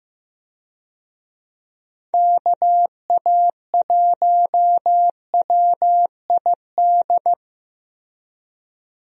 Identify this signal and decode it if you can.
Morse code, heard as a keyed tone, about 15 wpm, 700 Hz. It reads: KA1WID
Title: USA Call Signs - 6 Characters - Repeated 3x - 15wpm